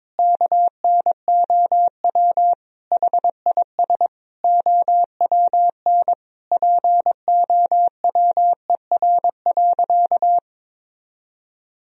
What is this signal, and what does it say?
Morse code, heard as a keyed tone, about 22 wpm, 700 Hz. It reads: KNOW HIS OWN POWER.
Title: Sentences from Top 1000 Words - Review Part 2 - 22wpm